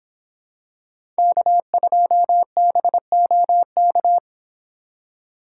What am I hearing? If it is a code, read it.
Morse code, heard as a keyed tone, about 26 wpm, 700 Hz. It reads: K2BOK
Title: Speed Racing - USA Call Signs 22wpm